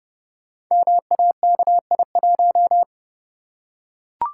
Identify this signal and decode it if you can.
Morse code, heard as a keyed tone, about 30 wpm, 700 Hz. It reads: MAKI1
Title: Code Groups - 5 Letter:Numbers - 30wpm